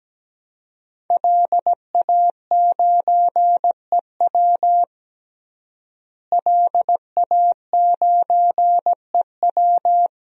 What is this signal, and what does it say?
Morse code, heard as a keyed tone, about 17 wpm, 700 Hz. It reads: LA9EW LA9EW
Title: DXCC-100 Call Signs - 5 Characters - Repeated 2x - 17wpm